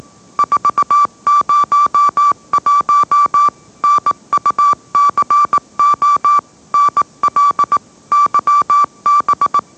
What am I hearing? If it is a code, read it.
Morse code, heard as a keyed tone, about 18 wpm, 1160 Hz. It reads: 401 NUCO NL YB
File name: mrx / mrx081 20wpm